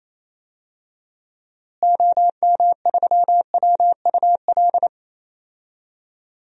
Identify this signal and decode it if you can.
Morse code, heard as a keyed tone, about 28 wpm, 700 Hz. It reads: OM3WUL
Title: DXCC-100 Call Signs - 6 Characters - Repeated 3x - 28wpm